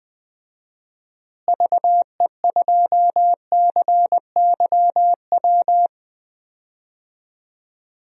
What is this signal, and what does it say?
Morse code, heard as a keyed tone, about 21 wpm, 700 Hz. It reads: VE2CYW
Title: DXCC-100 Call Signs - 6 Characters - Repeated 3x - 20wpm